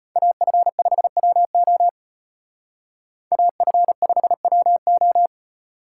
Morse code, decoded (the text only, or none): AF5WO AF5WO